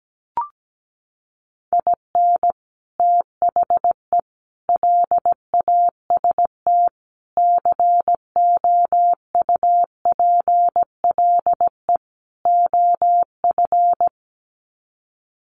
Morse code, heard as a keyed tone, about 17 wpm, 700 Hz. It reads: IN THE LAST COUPLE OF
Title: Sets of 5 Words 17wpm